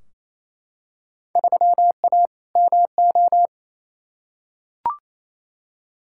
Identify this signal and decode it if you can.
Morse code, heard as a keyed tone, about 28 wpm, 700 Hz. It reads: 3A MO
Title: Field Day Exchanges Encore - 28wpm